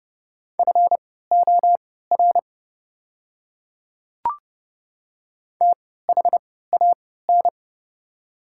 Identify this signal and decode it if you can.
Morse code, heard as a keyed tone, about 31 wpm, spaced as for 17 wpm, 690 Hz. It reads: FOR THAN
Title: 3x Character Spacing - Top 200 Words - 30wpm